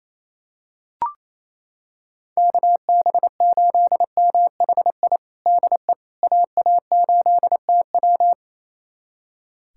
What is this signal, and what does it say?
Morse code, heard as a keyed tone, about 28 wpm, 700 Hz. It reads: KB8MHI DE AA8TW
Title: USA Callsign DE Callsign 28wpm